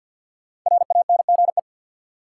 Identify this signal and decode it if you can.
Morse code, heard as a keyed tone, about 49 wpm, 700 Hz. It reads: RANGE
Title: Sentences from Top 900 Words - New Word Drill - 50wpm